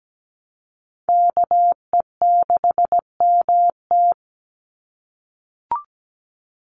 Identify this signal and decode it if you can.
Morse code, heard as a keyed tone, about 17 wpm, 700 Hz. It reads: KE6MT
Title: NA SOTA Chaser Callsigns 17wpm